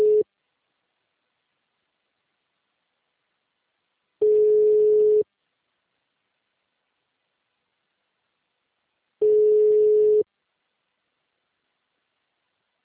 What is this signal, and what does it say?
Ringback tone (Europe, 425 Hz).
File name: out- -2003- - - .44636